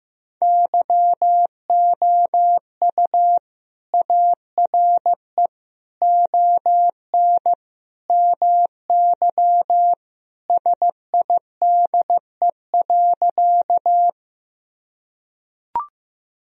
Morse code, heard as a keyed tone, about 16 wpm, 700 Hz. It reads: YOU ARE ON MY SIDE.
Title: Sentences from Top 1000 Words - Review Part 1 - 15wpm